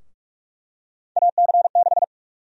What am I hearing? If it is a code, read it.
Morse code, heard as a keyed tone, about 46 wpm, 700 Hz. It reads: AC6
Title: USA Call Sign Prefixes - Repeated 2x - 45wpm